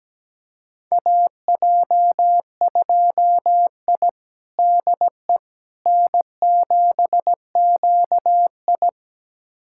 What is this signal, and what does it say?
Morse code, heard as a keyed tone, about 17 wpm, 700 Hz. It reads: AJ2I DE N7QI
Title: USA Callsign DE Callsign - 4 Characters - 17wpm